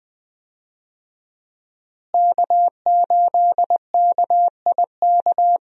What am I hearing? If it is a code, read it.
Morse code, heard as a keyed tone, about 20 wpm, 700 Hz. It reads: K8KIK
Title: USA Call Signs - 5 Characters - Repeated 2x - 20wpm